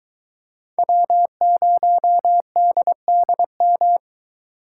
Morse code, heard as a keyed tone, about 23 wpm, 700 Hz. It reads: W0DDM